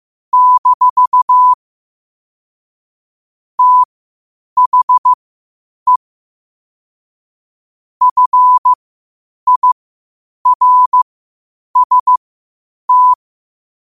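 Morse code, heard as a keyed tone, about 15 wpm, spaced as for 8 wpm, 1000 Hz. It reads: - THE FIRST
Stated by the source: Morse code at 8 wpm.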